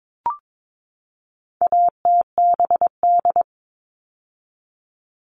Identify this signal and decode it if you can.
Morse code, heard as a keyed tone, about 22 wpm, 700 Hz. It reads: ATBD
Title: All Cut Numbers - 4-Digit 22wpm